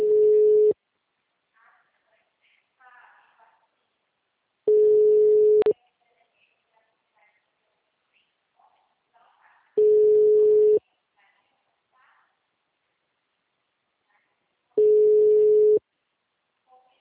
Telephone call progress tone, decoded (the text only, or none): ringback tone